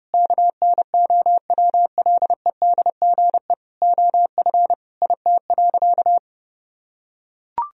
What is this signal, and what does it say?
Morse code, heard as a keyed tone, about 31 wpm, 700 Hz. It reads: KNOWLEDGE OF IT.
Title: Sentences from Top 1000 Words - Review Part 3 - 30wpm